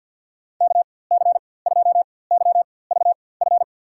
Morse code, heard as a keyed tone, about 52 wpm, spaced as for 23 wpm, 700 Hz. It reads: KC2YVF
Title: USA Call Signs - 4x Character Spacing - 50wpm